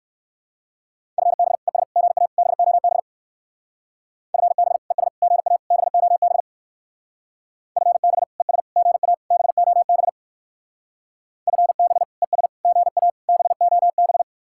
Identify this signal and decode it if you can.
Morse code, heard as a keyed tone, about 66 wpm, 700 Hz. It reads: FB ES GA BOB FB ES GA BOB FB ES GA BOB FB ES GA BOB